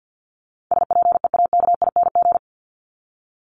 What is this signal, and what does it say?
Morse code, heard as a keyed tone, about 49 wpm, 700 Hz. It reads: SPEAKING
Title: ING Suffix - Top 100 Words with Base Words 50wpm